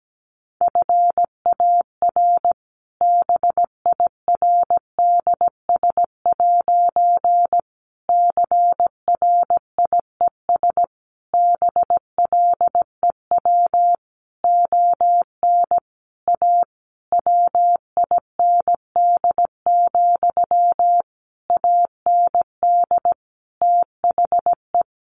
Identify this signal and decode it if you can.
Morse code, heard as a keyed tone, about 17 wpm, 700 Hz. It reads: FAR BIRDS' CRIES BLEW ON A WIND, AND THE
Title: A Sound Of Thunder 17wpm